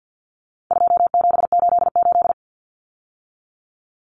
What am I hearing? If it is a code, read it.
Morse code, heard as a keyed tone, about 50 wpm, 700 Hz. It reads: 2788
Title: Common Cut Numbers - 4-Digit 50wpm